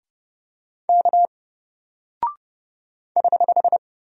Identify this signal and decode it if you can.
Morse code, heard as a keyed tone, about 31 wpm, 700 Hz. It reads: K <HH>
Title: Single Character (Letter:Number:Punctuation:Prosign) 30wpm